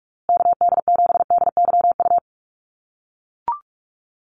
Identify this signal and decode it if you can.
Morse code, heard as a keyed tone, about 44 wpm, 700 Hz. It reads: KD7DYU
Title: USA Call Signs Repeated 3x 45wpm